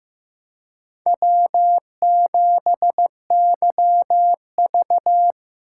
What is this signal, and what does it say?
Morse code, heard as a keyed tone, about 15 wpm, 700 Hz. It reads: W7YV
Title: USA Call Signs - 4 Characters - 3x Character Spacing - 15wpm